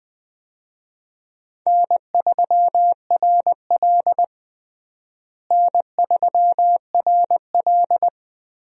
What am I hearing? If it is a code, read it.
Morse code, heard as a keyed tone, about 21 wpm, 700 Hz. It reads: N3RL N3RL
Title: USA Call Signs - 4 Characters - Repeated 3x - 20wpm